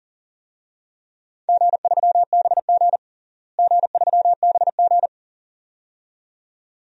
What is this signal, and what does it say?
Morse code, heard as a keyed tone, about 40 wpm, 700 Hz. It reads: G3BG G3BG
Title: DXCC-100 Call Signs - 4 Characters - Repeated 2x - 40wpm